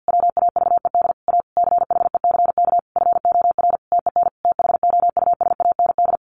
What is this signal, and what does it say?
Morse code, heard as a keyed tone, about 50 wpm, 700 Hz. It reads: WAVED A CHECK FOR TEN THOUSAND
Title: A Sound Of Thunder 50wpm